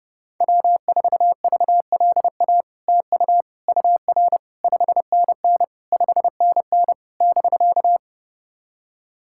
Morse code, heard as a keyed tone, about 30 wpm, 700 Hz. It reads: W4VLA TU UR 5NN 5NN <BK>
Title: SOTA QSO Practice 30wpm